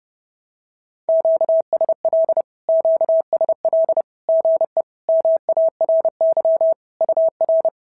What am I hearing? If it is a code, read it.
Morse code, heard as a keyed tone, about 30 wpm, 650 Hz. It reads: QSL QSL GE MARY UR
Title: SOTA QSO Practice 30wpm